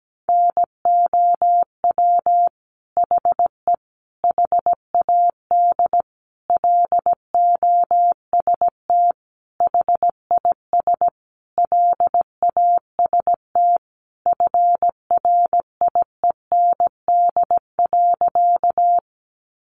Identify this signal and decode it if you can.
Morse code, heard as a keyed tone, about 17 wpm, 700 Hz. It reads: NOW HE HAD LOST HIS LAST FRIEND.
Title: Sentences from Top 1400 Words 17wpm